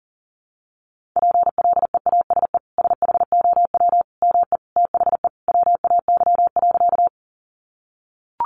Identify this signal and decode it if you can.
Morse code, heard as a keyed tone, about 40 wpm, 700 Hz. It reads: PLEASE SHOW ME THE WAY.
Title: Sentences from Top 1500 Words 40wpm